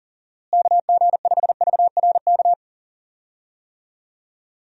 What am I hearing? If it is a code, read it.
Morse code, heard as a keyed tone, about 38 wpm, 700 Hz. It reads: KG5VRK
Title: USA Call Signs - 6 Characters - Repeated 2x - 40wpm